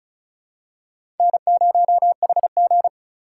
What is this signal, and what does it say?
Morse code, heard as a keyed tone, about 34 wpm, 700 Hz. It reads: N0HG
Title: USA Call Signs - 4 Characters - 4x Character Spacing - 35wpm